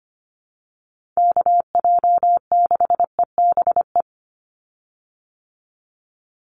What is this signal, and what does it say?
Morse code, heard as a keyed tone, about 25 wpm, 690 Hz. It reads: KJ6EBE
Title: USA Call Signs 25wpm